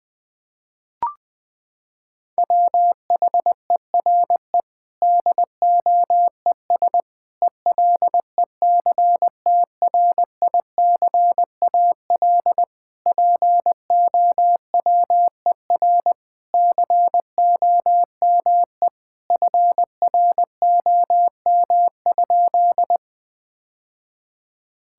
Morse code, heard as a keyed tone, about 20 wpm, 700 Hz. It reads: WHERE DOES ELECTRICAL POWER COME FROM?
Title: Sentences from Top 1000 Words - Review Part 6 - 20wpm